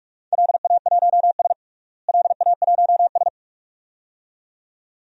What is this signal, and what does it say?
Morse code, heard as a keyed tone, about 46 wpm, 700 Hz. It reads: LA1S LA1S
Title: DXCC-100 Call Signs - 4 Characters - Repeated 2x - 45wpm